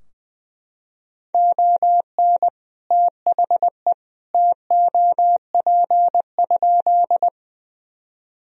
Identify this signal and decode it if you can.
Morse code, heard as a keyed tone, about 21 wpm, 700 Hz. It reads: ON THE TOP?
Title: Sentences from Top 1000 Words - Review Part 2 - 20wpm